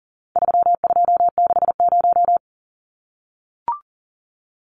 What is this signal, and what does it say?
Morse code, heard as a keyed tone, about 40 wpm, 700 Hz. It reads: 3260
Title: Numbers - 4-Digit 40wpm